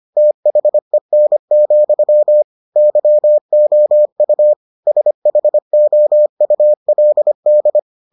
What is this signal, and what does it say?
Morse code, heard as a keyed tone, about 25 wpm, 600 Hz. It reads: THEN, YOU SHOULD